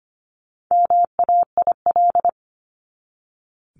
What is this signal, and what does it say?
Morse code, heard as a keyed tone, about 25 wpm, 700 Hz. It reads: MAIL
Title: Top 1751-2000 Words 25wpm